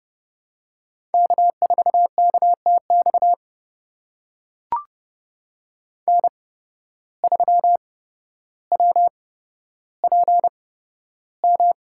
Morse code, heard as a keyed tone, about 32 wpm, spaced as for 13 wpm, 700 Hz. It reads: K4KTX N3WPM
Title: USA Call Signs - 5 Characters - 8x Character Spacing - 30wpm